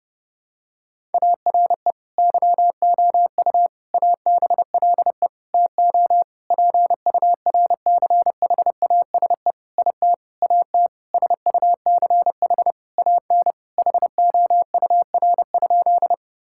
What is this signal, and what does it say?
Morse code, heard as a keyed tone, about 32 wpm, 700 Hz. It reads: ARE YOU ABLE TO PURCHASE IT AT SUCH AN HOUR?
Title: Sentences from Top 1000 Words - Review Part 5 - 30wpm